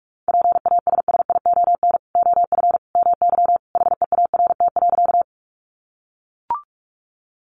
Morse code, heard as a keyed tone, about 44 wpm, 700 Hz. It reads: PASSION OF MY HEART.
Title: Sentences from Top 1300 Words 45wpm